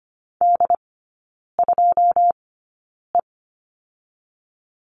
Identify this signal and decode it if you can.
Morse code, heard as a keyed tone, about 25 wpm, spaced as for 9 wpm, 690 Hz. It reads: D2E
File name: Lesson 14e - Number 2 - Three Characters 25wpm